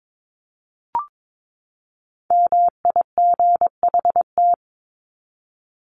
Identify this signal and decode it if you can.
Morse code, heard as a keyed tone, about 22 wpm, 700 Hz. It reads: MIGHT
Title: Top 300 Words 22wpm